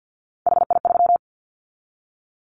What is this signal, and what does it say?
Morse code, heard as a keyed tone, about 50 wpm, 700 Hz. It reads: HI3
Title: Callsign Prefixes for Top 100 DXCC - 50wpm